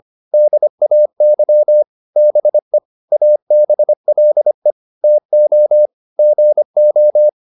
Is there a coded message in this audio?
Yes: Morse code from 0.3 to 7.3 s, heard as a keyed tone, about 25 wpm, 600 Hz. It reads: DAY BE ABLE TO GO